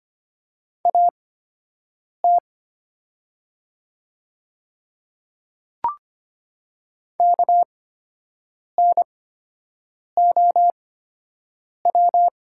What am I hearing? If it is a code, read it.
Morse code, heard as a keyed tone, about 25 wpm, spaced as for 7 wpm, 700 Hz. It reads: AT KNOW
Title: 8x Character Spacing - Top 100 Words - 25wpm